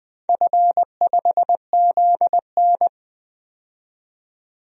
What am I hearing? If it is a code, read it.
Morse code, heard as a keyed tone, about 20 wpm, 700 Hz. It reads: F5ZN
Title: DXCC-100 Call Signs - 4 Characters - 20wpm